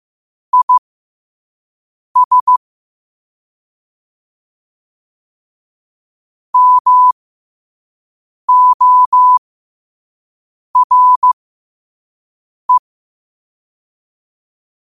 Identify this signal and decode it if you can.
Morse code, heard as a keyed tone, about 15 wpm, spaced as for 5 wpm, 1000 Hz. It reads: IS MORE
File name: qotdfiles / 5wpm